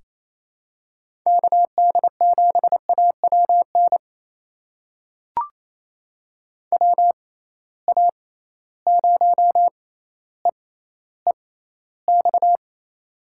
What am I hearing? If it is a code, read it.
Morse code, heard as a keyed tone, about 28 wpm, spaced as for 14 wpm, 700 Hz. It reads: KD7AWN WA0EEX
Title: USA Call Signs - 6 Characters - 6x Character Spacing - 28wpm